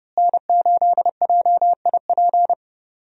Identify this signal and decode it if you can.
Morse code, heard as a keyed tone, about 30 wpm, 700 Hz. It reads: N8JIP
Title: USA Call Signs - 5 Characters - Repeated 2x - 30wpm